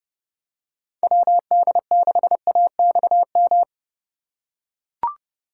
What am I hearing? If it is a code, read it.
Morse code, heard as a keyed tone, about 29 wpm, 700 Hz. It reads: WD6AXM